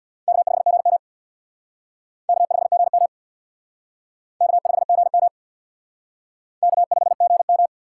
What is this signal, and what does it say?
Morse code, heard as a keyed tone, about 63 wpm, 700 Hz. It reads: K5GM K5GM K5GM K5GM